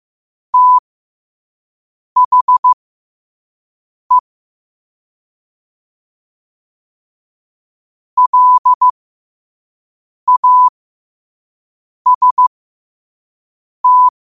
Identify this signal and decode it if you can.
Morse code, heard as a keyed tone, about 15 wpm, spaced as for 5 wpm, 1000 Hz. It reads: THE LAST
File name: qotdfiles / 5wpm